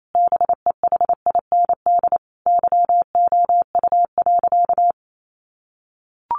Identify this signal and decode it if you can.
Morse code, heard as a keyed tone, about 28 wpm, 700 Hz. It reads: BEHIND YOU.
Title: Sentences from Top 500 Words 28wpm